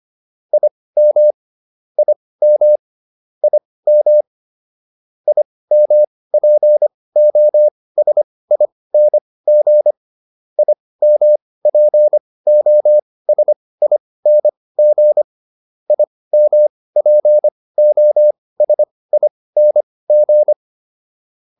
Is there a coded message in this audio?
Yes: Morse code from 0.5 to 20.5 s, heard as a keyed tone, about 25 wpm, spaced as for 18 wpm, 600 Hz. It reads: IM IM IM IMPOSING IMPOSING IMPOSING